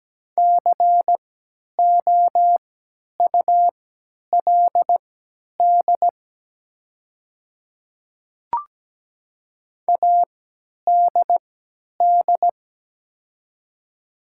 Morse code, heard as a keyed tone, about 17 wpm, spaced as for 10 wpm, 700 Hz. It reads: COULD ADD
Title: 3x Character Spacing - Top 300 Words - 17wpm